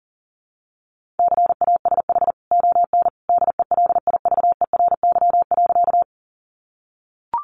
Morse code, heard as a keyed tone, about 40 wpm, 700 Hz. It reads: CASH ON DELIVERY.
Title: Sentences from Top 1500 Words 40wpm